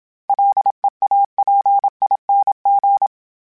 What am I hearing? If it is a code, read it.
Morse code, heard as a keyed tone, about 27 wpm, 800 Hz. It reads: LEAPING
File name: Special INGs-mp3s / Special INGs-mp3s / Sngl Ing @25